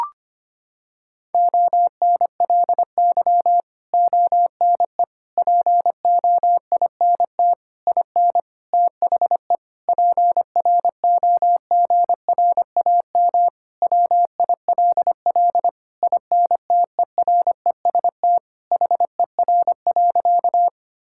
Morse code, heard as a keyed tone, about 25 wpm, 700 Hz. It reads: ONLY ONE POINT IN THE PROGRAM WILL INTEREST HER.
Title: Sentences from Top 1000 Words - Review Part 5 - 25wpm